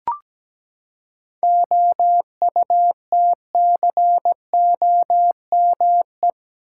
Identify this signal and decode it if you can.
Morse code, heard as a keyed tone, about 17 wpm, 700 Hz. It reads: OUTCOME
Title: Seven Letter Words - 17wpm